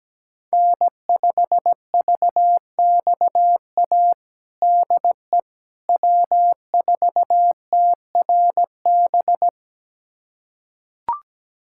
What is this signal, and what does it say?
Morse code, heard as a keyed tone, about 17 wpm, 700 Hz. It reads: N5VXA DE W4TRB
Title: USA Callsign DE Callsign - 5 Characters - 17wpm